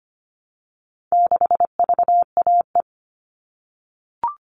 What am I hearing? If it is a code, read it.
Morse code, heard as a keyed tone, about 25 wpm, 700 Hz. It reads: 6VAE
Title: All Cut Numbers - 4-Digit 25wpm